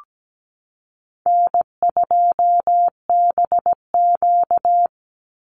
Morse code, heard as a keyed tone, about 17 wpm, 700 Hz. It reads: N2BQ
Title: USA Call Signs 17wpm